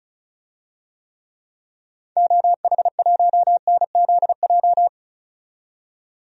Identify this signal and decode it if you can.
Morse code, heard as a keyed tone, about 36 wpm, 700 Hz. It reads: OH1NZJ
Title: DXCC-100 Call Signs - 6 Characters - 35wpm